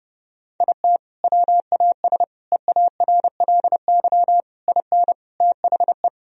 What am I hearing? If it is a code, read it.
Morse code, heard as a keyed tone, about 30 wpm, 700 Hz. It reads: IT WAS EARLY IN THE
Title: Sentences from Top 1000 Words - Review Part 2 - 30wpm